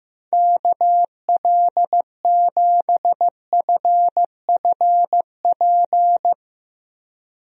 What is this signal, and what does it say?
Morse code, heard as a keyed tone, about 15 wpm, 700 Hz. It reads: KL7FFP
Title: USA Call Signs - 6 Characters - Repeated 3x - 15wpm